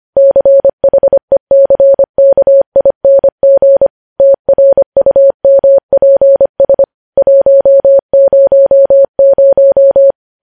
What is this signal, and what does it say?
Morse code, heard as a keyed tone, about 25 wpm, 550 Hz. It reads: CHECKING TRUMPS 100
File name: podcast / News headlines-1x-25x25